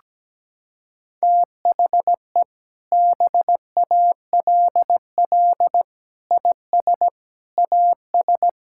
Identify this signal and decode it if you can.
Morse code, heard as a keyed tone, about 17 wpm, 700 Hz. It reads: THE BALL IS AS
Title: Sentences from Top 1000 Words - Review Part 6 - 17wpm